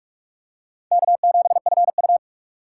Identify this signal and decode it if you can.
Morse code, heard as a keyed tone, about 47 wpm, 700 Hz. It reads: K7FU
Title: USA Call Signs - 4 Characters - 3x Character Spacing - 45wpm